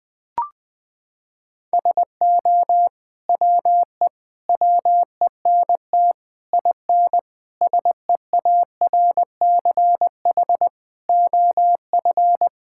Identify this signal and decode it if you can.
Morse code, heard as a keyed tone, about 20 wpm, 700 Hz. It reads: SO WE WENT IN SEARCH OF
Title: Sentences from Top 1000 Words - Review Part 5 - 20wpm